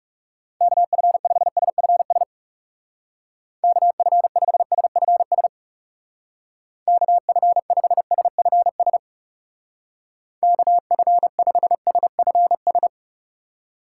Morse code, heard as a keyed tone, about 36 wpm, 700 Hz. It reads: KF5SFS KF5SFS KF5SFS KF5SFS
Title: Speed Racing - USA 2x3 Call Signs 30wpm